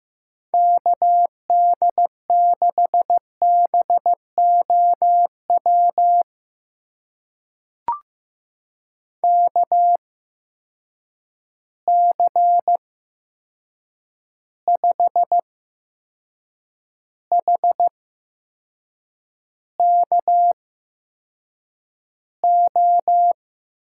Morse code, heard as a keyed tone, about 15 wpm, spaced as for 6 wpm, 700 Hz. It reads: KD6BOW KC5HKO
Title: USA Call Signs - 6 Characters - 8x Character Spacing - 15wpm